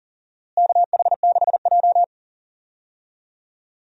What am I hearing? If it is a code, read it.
Morse code, heard as a keyed tone, about 40 wpm, 700 Hz. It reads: KH6J
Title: USA Call Signs - 4 Characters - Repeated 3x - 40wpm